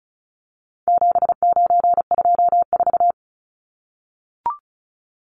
Morse code, heard as a keyed tone, about 34 wpm, 700 Hz. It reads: 7924
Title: Numbers - 4-Digit 35wpm